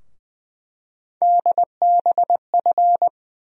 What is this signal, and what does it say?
Morse code, heard as a keyed tone, about 21 wpm, 700 Hz. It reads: DBF
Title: USA Call Sign Postfixes - 3 Letters 20wpm